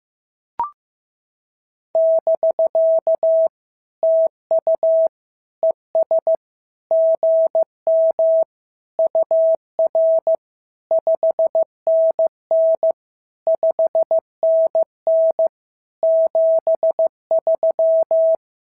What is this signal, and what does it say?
Morse code, heard as a keyed tone, about 14 wpm, 650 Hz. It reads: <BK> TU ES GM UR 5NN 5NN 73
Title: SOTA QSO Practice 15wpm